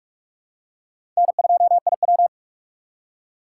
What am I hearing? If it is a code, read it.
Morse code, heard as a keyed tone, about 45 wpm, 700 Hz. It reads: N2IW
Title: USA Call Signs - 4 Characters - 6x Character Spacing - 45wpm